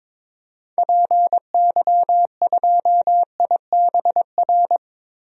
Morse code, heard as a keyed tone, about 21 wpm, 700 Hz. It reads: PY2IBR
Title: DXCC-100 Call Signs - 6 Characters - Repeated 2x - 22wpm